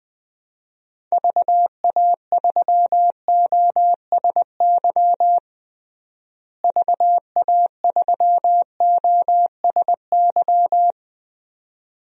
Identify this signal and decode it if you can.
Morse code, heard as a keyed tone, about 21 wpm, 700 Hz. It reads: VA3OSY VA3OSY
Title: DXCC-100 Call Signs - 6 Characters - Repeated 2x - 20wpm